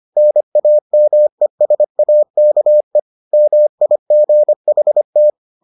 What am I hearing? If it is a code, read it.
Morse code, heard as a keyed tone, about 25 wpm, 600 Hz. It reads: NAMESAKE MIGHT